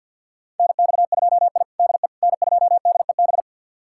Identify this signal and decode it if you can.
Morse code, heard as a keyed tone, about 51 wpm, 700 Hz. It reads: NX2I DE N2DEB
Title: USA Callsign DE Callsign 50wpm